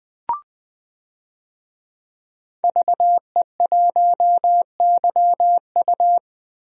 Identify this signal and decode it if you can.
Morse code, heard as a keyed tone, about 21 wpm, 700 Hz. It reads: VE1YU
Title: DXCC-100 Call Signs - 5 Characters - 20wpm